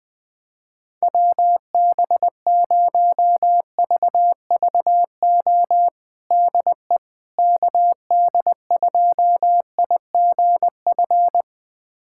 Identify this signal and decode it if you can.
Morse code, heard as a keyed tone, about 20 wpm, 700 Hz. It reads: WB0VVO DE KD2IGF
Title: USA Callsign DE Callsign - 6 Characters - 20wpm